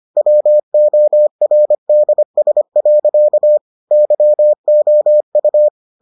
Morse code, heard as a keyed tone, about 25 wpm, 600 Hz. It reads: WORDS. YOU